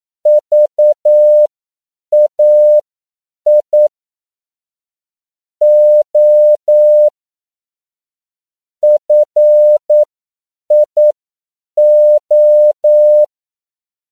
Morse code, heard as a keyed tone, about 9 wpm, spaced as for 7 wpm, 600 Hz. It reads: VAI O FIO
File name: CW / 10PPM